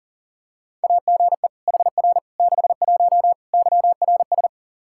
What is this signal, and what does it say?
Morse code, heard as a keyed tone, about 40 wpm, 700 Hz. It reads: AGE HR 61 YRS
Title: Practice QSO - Exchange 3 - More Info - 40wpm